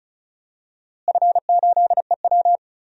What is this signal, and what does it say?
Morse code, heard as a keyed tone, about 36 wpm, 700 Hz. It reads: F8EW
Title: DXCC-100 Call Signs - 4 Characters - Repeated 3x - 35wpm